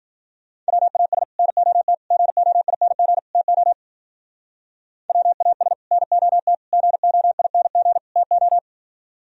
Morse code, heard as a keyed tone, about 48 wpm, 700 Hz. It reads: WAS NOT GOING TO WAS NOT GOING TO
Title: Speed-Racing - Sets of 4 Words 40wpm